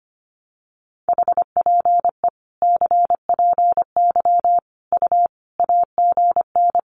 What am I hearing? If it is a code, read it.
Morse code, heard as a keyed tone, about 25 wpm, 700 Hz. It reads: HPE CPY U AGN
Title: Common QSO Phrases 25wpm